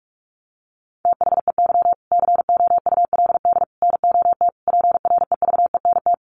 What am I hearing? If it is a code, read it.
Morse code, heard as a keyed tone, about 46 wpm, 700 Hz. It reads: THEY COULD NOT PREVENT
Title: Sentences from Top 1100 Words 45wpm